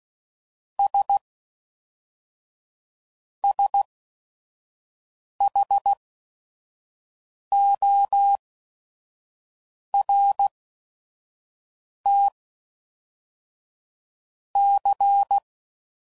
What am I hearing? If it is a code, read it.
Morse code, heard as a keyed tone, about 16 wpm, 800 Hz. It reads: S S H O R T C